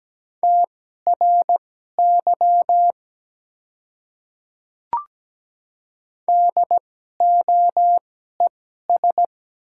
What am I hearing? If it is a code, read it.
Morse code, heard as a keyed tone, about 17 wpm, spaced as for 12 wpm, 700 Hz. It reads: TRY DOES
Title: 2x Character Spacing - Top 200 Words - 17wpm